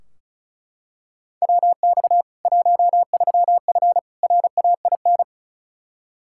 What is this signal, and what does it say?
Morse code, heard as a keyed tone, about 36 wpm, 700 Hz. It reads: WX 13F RAIN
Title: K1USN SST QSO Practice 35wpm